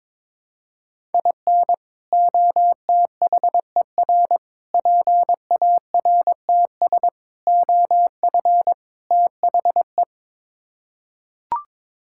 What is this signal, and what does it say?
Morse code, heard as a keyed tone, about 22 wpm, 700 Hz. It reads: IN OTHER PARTS OF THE
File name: practice / 22wpm / Build A Phrase - 5-Word Phrases - 22wpm